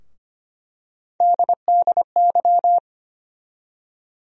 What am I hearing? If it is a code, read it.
Morse code, heard as a keyed tone, about 25 wpm, 700 Hz. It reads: DDY
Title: USA Call Sign Postfixes - 3 Letters 25wpm